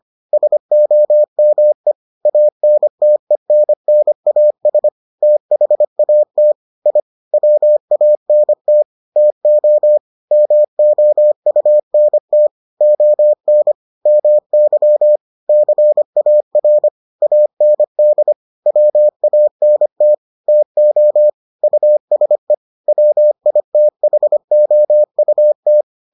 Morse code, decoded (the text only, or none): SOME ANTENNAS THAT I WANT TO MOUNT ON MY CAR AND WANT TO USE WITHOUT